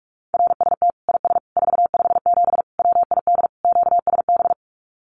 Morse code, heard as a keyed tone, about 44 wpm, 700 Hz. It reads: RST IS 457 WID QSB
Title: Common QSO Phrases 45wpm